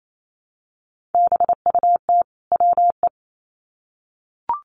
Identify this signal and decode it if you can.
Morse code, heard as a keyed tone, about 28 wpm, 700 Hz. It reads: BUT WE
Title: Sets of 2 Words 28wpm